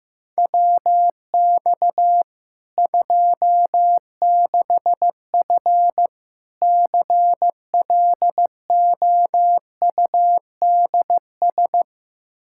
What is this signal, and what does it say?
Morse code, heard as a keyed tone, about 15 wpm, 700 Hz. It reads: WX 26F CLOUDS
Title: Practice QSO - Exchange 2 - Info - 15wpm